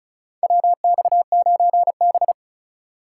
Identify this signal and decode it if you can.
Morse code, heard as a keyed tone, about 34 wpm, 700 Hz. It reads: WX9B
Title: USA Call Signs - 2x Character Spacing - 35wpm